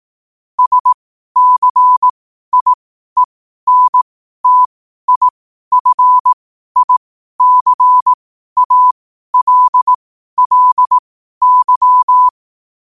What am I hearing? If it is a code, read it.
Morse code, heard as a keyed tone, about 18 wpm, spaced as for 12 wpm, 1000 Hz. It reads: SCIENTIFICALLY